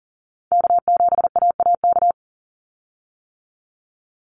Morse code, heard as a keyed tone, about 40 wpm, 700 Hz. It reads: K7AAK
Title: USA Call Signs 40wpm